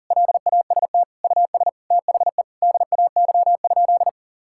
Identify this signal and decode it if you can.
Morse code, heard as a keyed tone, about 40 wpm, 700 Hz. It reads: LAST US THE DAY?
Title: Sentences from Top 1000 Words - Review Part 5 - 40wpm